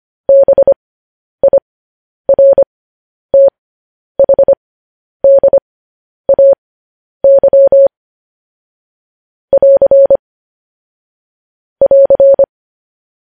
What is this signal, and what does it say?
Morse code, heard as a keyed tone, about 25 wpm, spaced as for 10 wpm, 550 Hz. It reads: BIRTHDAY <AR> <AR>
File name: podcast / News headlines-1x-25x10